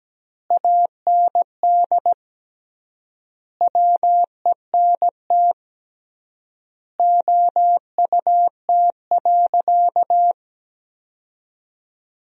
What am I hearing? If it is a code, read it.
Morse code, heard as a keyed tone, about 17 wpm, 700 Hz. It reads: AND WENT OUT.
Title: 3x Word-Spacing - Sentences from Top 300 Words - 17wpm